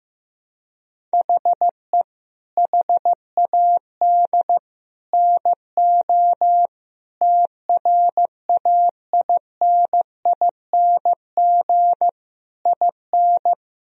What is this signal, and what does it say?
Morse code, heard as a keyed tone, about 16 wpm, 700 Hz. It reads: HE HAD NO TRAINING IN
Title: Sentences from Top 1000 Words - Review Part 4 - 15wpm